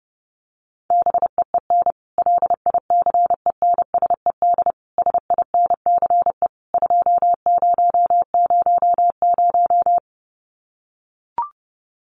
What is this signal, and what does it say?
Morse code, heard as a keyed tone, about 30 wpm, 700 Hz. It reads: BEEN LICENSED SINCE 2000
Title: Common QSO Phrases 30wpm